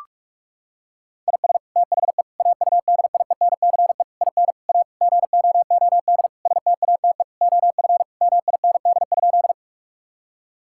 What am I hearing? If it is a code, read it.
Morse code, heard as a keyed tone, about 42 wpm, 700 Hz. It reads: IS THE AUDIENCE IN A GOOD STATE OF MIND?
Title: Sentences from Top 1000 Words - Review Part 5 - 45wpm